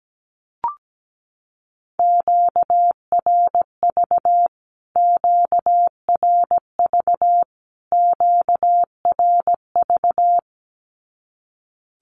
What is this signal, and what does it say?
Morse code, heard as a keyed tone, about 17 wpm, 700 Hz. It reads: QRV QRV QRV
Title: Q-Codes Repeated 3x 17wpm